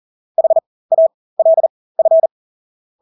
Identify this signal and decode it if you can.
Morse code, heard as a keyed tone, about 40 wpm, spaced as for 20 wpm, 650 Hz. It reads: HALF
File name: Engelske ord 40-20 wpm